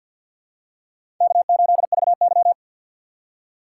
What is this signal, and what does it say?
Morse code, heard as a keyed tone, about 53 wpm, 700 Hz. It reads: K8VY
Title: USA Call Signs - 3x Character Spacing - 50wpm